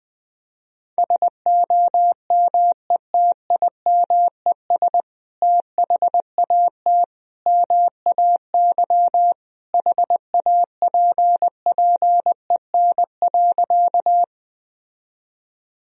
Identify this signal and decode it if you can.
Morse code, heard as a keyed tone, about 21 wpm, 700 Hz. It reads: SOMETIMES THAT MAY HAPPEN.
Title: Sentences from Top 1000 Words - Review Part 3 - 20wpm